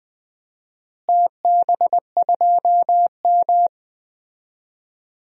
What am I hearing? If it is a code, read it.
Morse code, heard as a keyed tone, about 21 wpm, 700 Hz. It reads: TB2M
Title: DXCC-100 Call Signs - 4 Characters - Repeated 2x - 20wpm